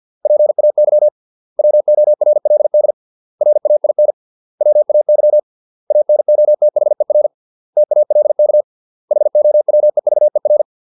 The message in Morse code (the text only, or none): PAY WORLD RAIN WAY ANOTHER TALK HOWEVER